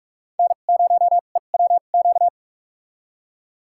Morse code, heard as a keyed tone, about 44 wpm, spaced as for 32 wpm, 700 Hz. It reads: N0EWQ
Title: USA Call Signs - 5 Characters - 2x Character Spacing - 45wpm